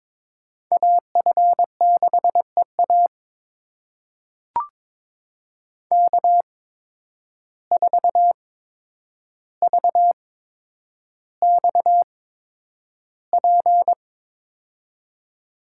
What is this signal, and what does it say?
Morse code, heard as a keyed tone, about 22 wpm, spaced as for 9 wpm, 700 Hz. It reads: AF6EA K4VXP
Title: USA Call Signs - 5 Characters - 8x Character Spacing - 22wpm